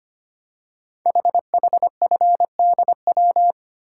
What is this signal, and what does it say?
Morse code, heard as a keyed tone, about 25 wpm, 700 Hz. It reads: HHFDW
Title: Code Groups - 5 Letters - 25wpm